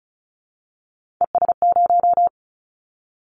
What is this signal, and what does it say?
Morse code, heard as a keyed tone, about 34 wpm, 700 Hz. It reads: ES0
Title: Callsign Prefixes for Top 100 DXCC - 35wpm